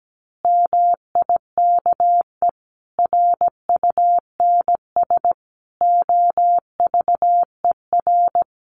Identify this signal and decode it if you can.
Morse code, heard as a keyed tone, about 17 wpm, 700 Hz. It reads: MIKE RUNS OVER
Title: Additional Easy Phrases 17wpm